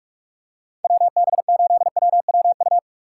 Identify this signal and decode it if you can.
Morse code, heard as a keyed tone, about 38 wpm, 700 Hz. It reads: WB8WWU